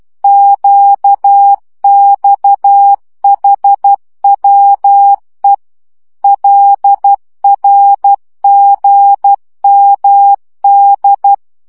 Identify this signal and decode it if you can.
Morse code, heard as a keyed tone, about 12 wpm, 800 Hz. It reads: QXHWE LRGMD